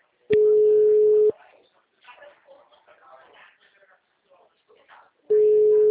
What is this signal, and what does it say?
Ringback tone (Europe, 425 Hz), with speech.
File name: out- -2008- - - .66087